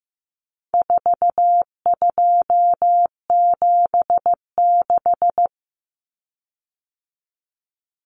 Morse code, heard as a keyed tone, about 15 wpm, 700 Hz. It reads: 4276
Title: Common Cut Numbers - 4-Digit 15wpm